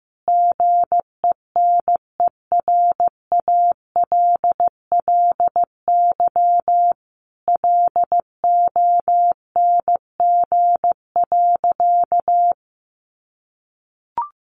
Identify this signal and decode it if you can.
Morse code, heard as a keyed tone, about 15 wpm, 700 Hz. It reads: GENERALLY LONG.
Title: Sentences from Top 600 Words 15wpm